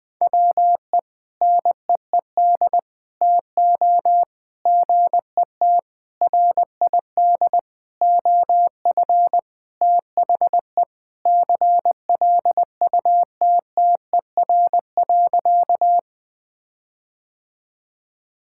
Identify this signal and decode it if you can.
Morse code, heard as a keyed tone, about 20 wpm, 700 Hz. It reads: WE NEED TO GET RID OF THE CLUTTER.
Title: Sentences with 5-Word Phrases - 20wpm